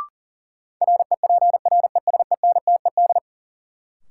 Morse code, heard as a keyed tone, about 40 wpm, 700 Hz. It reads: REPRESENTED
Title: Sets of 5 Words - New Word Drill - 40wpm